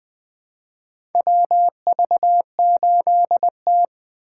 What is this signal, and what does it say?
Morse code, heard as a keyed tone, about 20 wpm, 700 Hz. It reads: WV8T
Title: USA Call Signs - 4 Characters - 4x Character Spacing - 20wpm